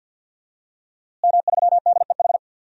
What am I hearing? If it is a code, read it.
Morse code, heard as a keyed tone, about 51 wpm, 700 Hz. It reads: M3DEH
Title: DXCC-100 Call Signs - 5 Characters - Repeated 2x - 50wpm